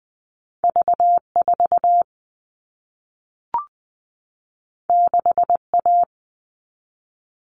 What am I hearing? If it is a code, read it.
Morse code, heard as a keyed tone, about 20 wpm, 700 Hz. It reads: V 4 6 A
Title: All Cut Numbers - 2-Digit 20wpm